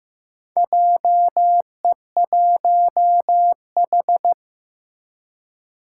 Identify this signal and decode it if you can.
Morse code, heard as a keyed tone, about 15 wpm, 700 Hz. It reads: JE1H
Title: Four Challenging Letter-Number Combinations 15wpm